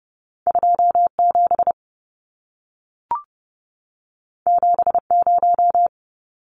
Morse code, heard as a keyed tone, about 30 wpm, 700 Hz. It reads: 27 70
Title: Numbers - 2-Digit 30wpm